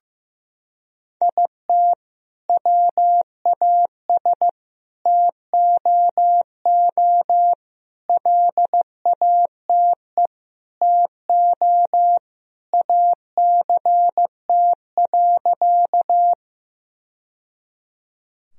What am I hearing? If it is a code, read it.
Morse code, heard as a keyed tone, about 15 wpm, 700 Hz. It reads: IT WAS TOO LATE TO ACT.
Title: Sentences from Top 1000 Words - Review Part 1 - 15wpm